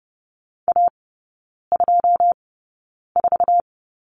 Morse code, heard as a keyed tone, about 30 wpm, spaced as for 9 wpm, 700 Hz. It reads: A24
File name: 081 Lesson 14e - Number 2 - Three Characters 30wpm